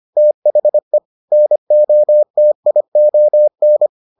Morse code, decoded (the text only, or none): THE NOTION